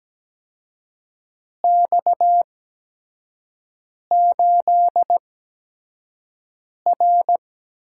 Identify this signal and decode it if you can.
Morse code, heard as a keyed tone, about 17 wpm, spaced as for 5 wpm, 700 Hz. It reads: X8R
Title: USA Call Signs - 4 Characters - 8x Character Spacing - 17wpm